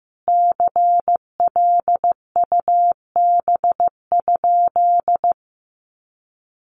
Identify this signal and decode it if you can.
Morse code, heard as a keyed tone, about 15 wpm, 700 Hz. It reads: CLUB?
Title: Sentences from Top 1200 Words 15wpm